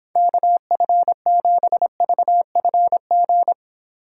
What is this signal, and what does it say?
Morse code, heard as a keyed tone, about 26 wpm, 700 Hz. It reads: KF7VFG